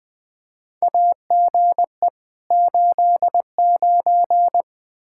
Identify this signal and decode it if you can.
Morse code, heard as a keyed tone, about 21 wpm, 700 Hz. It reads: AGE 89
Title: K1USN SST QSO Practice 20wpm